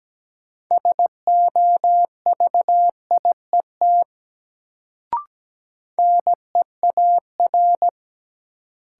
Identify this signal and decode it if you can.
Morse code, heard as a keyed tone, about 17 wpm, 700 Hz. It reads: SOVIET NEAR
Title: Sets of 5 Words Encore - New Word Drill - 17wpm